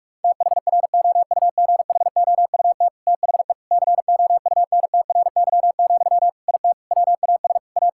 Morse code, heard as a keyed tone, about 41 wpm, 700 Hz. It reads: THROUGHOUT THE COUNTRY, IT WAS A
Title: Sentences from Top 1000 Words - Review Part 5 - 45wpm